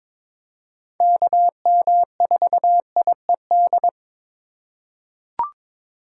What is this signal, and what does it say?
Morse code, heard as a keyed tone, about 22 wpm, 700 Hz. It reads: KM4IED
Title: USA Call Signs - 6 Characters - 3x Character Spacing - 22wpm